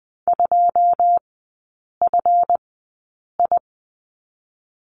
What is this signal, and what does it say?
Morse code, heard as a keyed tone, about 20 wpm, spaced as for 8 wpm, 700 Hz. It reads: 2FI
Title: 138 Lesson 23e - Letter f - Three Characters 20wpm